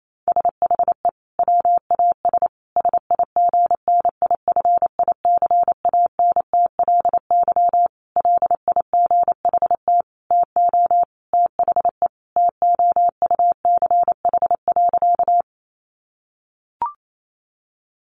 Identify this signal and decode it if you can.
Morse code, heard as a keyed tone, about 28 wpm, 700 Hz. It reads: SHE WAS SIGNIFICANTLY LIGHT TO THE TOUCH.
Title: Sentences from Top 1500 Words 28wpm